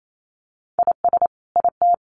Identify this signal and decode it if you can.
Morse code, heard as a keyed tone, about 28 wpm, 700 Hz. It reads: IS IT
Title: Sentences from Top 1500 Words 28wpm